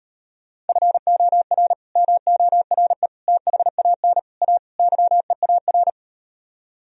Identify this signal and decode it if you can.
Morse code, heard as a keyed tone, about 38 wpm, 700 Hz. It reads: FOR MORE THAN A YEAR